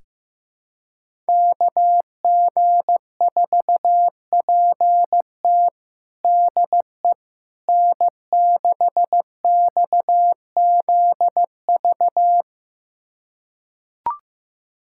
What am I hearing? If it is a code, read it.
Morse code, heard as a keyed tone, about 15 wpm, 700 Hz. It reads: KG4PT DE N6XZV
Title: USA Callsign DE Callsign - 5 Characters - 15wpm